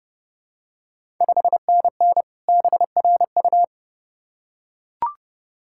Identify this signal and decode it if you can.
Morse code, heard as a keyed tone, about 32 wpm, 700 Hz. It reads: 5NN BRU
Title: Ontario QSO Party Exchange 30wpm